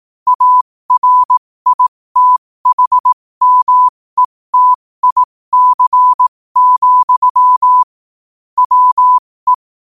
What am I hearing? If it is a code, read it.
Morse code, heard as a keyed tone, about 18 wpm, spaced as for 15 wpm, 1000 Hz. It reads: ARITHMETIC, WE
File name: qotdfiles / 15wpm